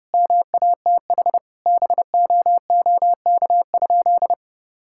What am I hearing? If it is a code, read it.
Morse code, heard as a keyed tone, about 32 wpm, 700 Hz. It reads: MATH BOOK?
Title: Sentences from Top 1000 Words - Review Part 6 - 30wpm